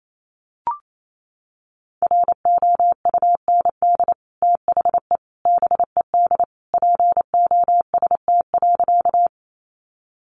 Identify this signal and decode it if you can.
Morse code, heard as a keyed tone, about 28 wpm, 700 Hz. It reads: ROUND THE BED POST.
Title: Sentences from Top 1000 Words 28wpm